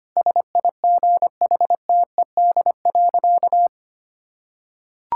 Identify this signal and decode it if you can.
Morse code, heard as a keyed tone, about 25 wpm, 700 Hz. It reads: SIGHTED.
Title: Pangram Challenge v2 - 25wpm